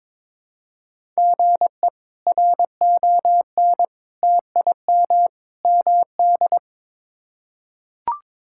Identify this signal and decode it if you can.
Morse code, heard as a keyed tone, about 22 wpm, 700 Hz. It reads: GE RON TIM MD
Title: K1USN SST QSO Practice - 22wpm